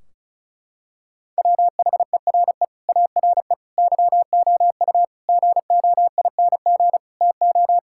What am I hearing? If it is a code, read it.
Morse code, heard as a keyed tone, about 34 wpm, 700 Hz. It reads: WHERE ARE YOU GOING TO
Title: Sentences from Top 1600 Words 35wpm